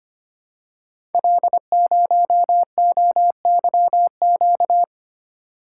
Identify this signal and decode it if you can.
Morse code, heard as a keyed tone, about 25 wpm, 700 Hz. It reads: L0OYQ
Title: Code Groups - 5 Letter:Numbers - 25wpm